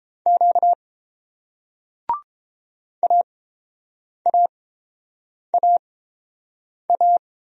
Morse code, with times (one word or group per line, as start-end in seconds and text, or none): 0.3-0.7 Q
3.0-7.2 AAAA